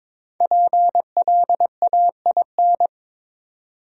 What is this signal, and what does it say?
Morse code, heard as a keyed tone, about 22 wpm, 700 Hz. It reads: PLAIN
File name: 5 Letter Words - First 3 Letters Given - 22wpm